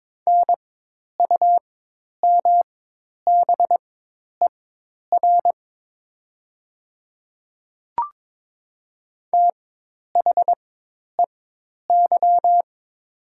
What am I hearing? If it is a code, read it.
Morse code, heard as a keyed tone, about 22 wpm, spaced as for 10 wpm, 700 Hz. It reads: NUMBER THEY
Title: 4x Character Spacing - Top 100 Words - 22wpm